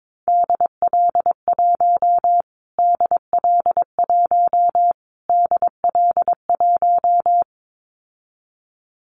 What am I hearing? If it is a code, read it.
Morse code, heard as a keyed tone, about 22 wpm, 700 Hz. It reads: DL1 DL1 DL1
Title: Callsign Prefixes for Top 25 DXCC - Repeated 3x - 22wpm